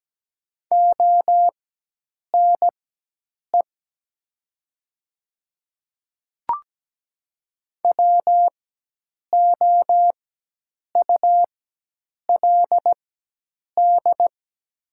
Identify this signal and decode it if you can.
Morse code, heard as a keyed tone, about 17 wpm, spaced as for 8 wpm, 700 Hz. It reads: ONE WOULD
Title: 4x Character Spacing - Top 100 Words - 17wpm